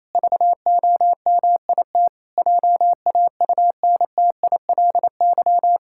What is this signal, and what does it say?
Morse code, heard as a keyed tone, about 28 wpm, 700 Hz. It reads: VOMIT JAUNTILY
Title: Pangram Challenge v2 - 28wpm